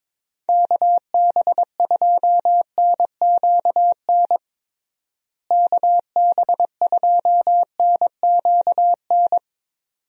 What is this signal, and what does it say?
Morse code, heard as a keyed tone, about 22 wpm, 700 Hz. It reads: KB2NQN KB2NQN
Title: USA Call Signs - 6 Characters - Repeated 2x - 22wpm